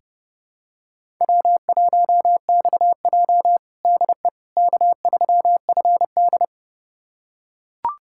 Morse code, heard as a keyed tone, about 30 wpm, 700 Hz. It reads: W1XJ DE K3FD
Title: USA Callsign DE Callsign - 4 Characters - 30wpm